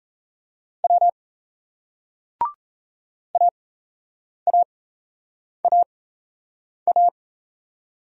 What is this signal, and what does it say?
Morse code, heard as a keyed tone, about 35 wpm, spaced as for 8 wpm, 700 Hz. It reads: W AAAA
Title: Speed-Racing Single Letters 28wpm